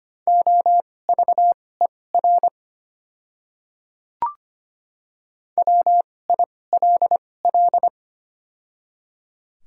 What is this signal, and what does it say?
Morse code, heard as a keyed tone, about 25 wpm, spaced as for 18 wpm, 700 Hz. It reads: OVER WILL
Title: 2x Character Spacing - Top 100 Words - 25wpm